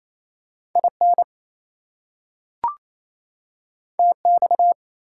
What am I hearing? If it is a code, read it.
Morse code, heard as a keyed tone, about 28 wpm, 700 Hz. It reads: IN TX
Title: US States and Canadian Provinces 28wpm